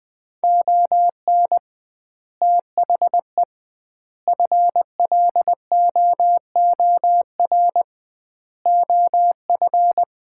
Morse code, heard as a keyed tone, about 20 wpm, 700 Hz. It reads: ON THE FLOOR OF
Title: 2x Word-Spacing - Sets of 5 Words - 20wpm